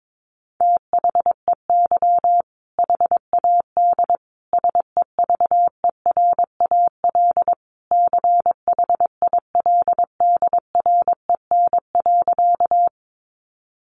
Morse code, heard as a keyed tone, about 22 wpm, 700 Hz. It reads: THEY HAD SEVERAL CHILDREN.
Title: Sentences from Top 400 Words 22wpm